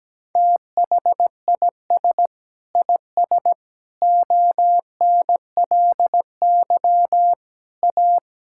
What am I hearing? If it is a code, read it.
Morse code, heard as a keyed tone, about 17 wpm, 700 Hz. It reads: THIS IS ONLY A
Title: Sentences from Top 1000 Words - Review Part 5 - 17wpm